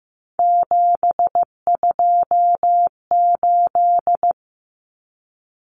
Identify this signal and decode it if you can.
Morse code, heard as a keyed tone, about 15 wpm, 700 Hz. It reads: 728
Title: Common Cut Numbers - 3-Digit 15wpm